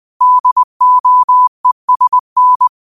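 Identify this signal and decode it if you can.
Morse code, heard as a keyed tone, about 20 wpm, 1000 Hz. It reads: DOESN
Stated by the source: Morse quote practice at 20 wpm.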